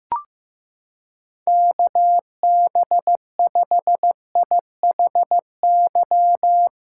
Morse code, heard as a keyed tone, about 15 wpm, 700 Hz. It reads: KB5IHY
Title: USA Callsign DE Callsign - 6 Characters - 15wpm